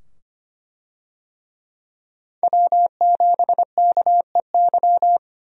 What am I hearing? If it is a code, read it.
Morse code, heard as a keyed tone, about 25 wpm, 700 Hz. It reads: W7KEY
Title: USA Call Signs - 5 Characters - 25wpm